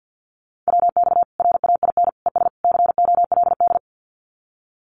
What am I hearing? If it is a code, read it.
Morse code, heard as a keyed tone, about 50 wpm, 700 Hz. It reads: WX RAIN ES COLD
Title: Common QSO Phrases 50wpm